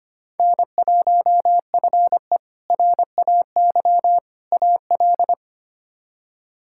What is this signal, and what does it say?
Morse code, heard as a keyed tone, about 25 wpm, 700 Hz. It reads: N1FE RAY AL
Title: K1USN SST QSO Practice - 25wpm